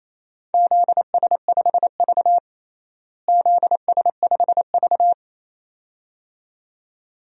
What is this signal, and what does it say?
Morse code, heard as a keyed tone, about 28 wpm, 700 Hz. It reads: ZS5V ZS5V
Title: DXCC-100 Call Signs - 4 Characters - Repeated 2x - 28wpm